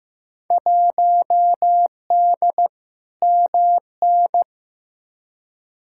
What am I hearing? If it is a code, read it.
Morse code, heard as a keyed tone, about 15 wpm, 700 Hz. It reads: 1D MN
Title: Field Day Exchanges - 15wpm